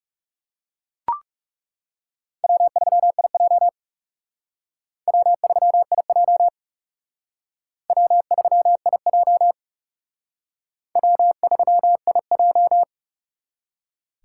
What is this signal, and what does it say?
Morse code, heard as a keyed tone, about 37 wpm, 700 Hz. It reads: W3IJ W3IJ W3IJ W3IJ
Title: Speed Racing - USA Call Signs 30wpm